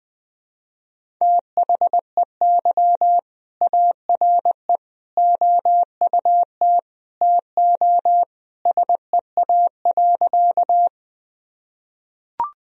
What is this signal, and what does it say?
Morse code, heard as a keyed tone, about 20 wpm, 700 Hz. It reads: THEY ARE OUT TO SEA.
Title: Sentences from Top 1000 Words - Review Part 1 - 20wpm